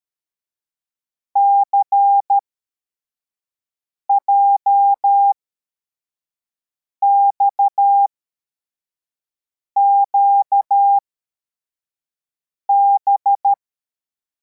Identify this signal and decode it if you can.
Morse code, heard as a keyed tone, about 13 wpm, spaced as for 4 wpm, 800 Hz. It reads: CJXQB